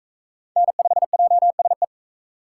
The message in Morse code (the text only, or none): N5JSE